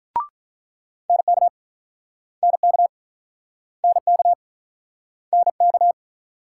Morse code, heard as a keyed tone, about 44 wpm, 700 Hz. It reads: NK NK NK NK